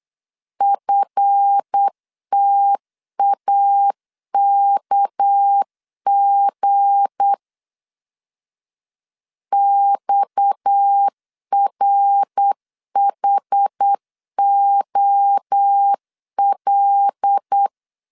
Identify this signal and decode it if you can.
Morse code, heard as a keyed tone, about 8 wpm, 800 Hz. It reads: FTAKG XRHOL
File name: LEZIONI / 7 GF9Z / 7 GF 8wpm 3108